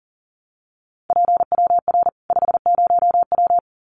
Morse code, heard as a keyed tone, about 40 wpm, 700 Hz. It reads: PWR 50W
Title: Common QSO Phrases 40wpm